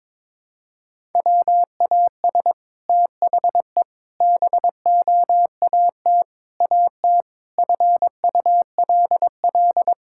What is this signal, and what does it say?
Morse code, heard as a keyed tone, about 21 wpm, 700 Hz. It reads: WAS THE BOAT AT FULL
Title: Sentences from Top 1000 Words - Review Part 5 - 22wpm